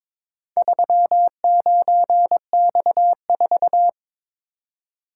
Morse code, heard as a keyed tone, about 22 wpm, 700 Hz. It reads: 39X4
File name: practice / 22wpm / Four Letter-Number Combinations - 22wpm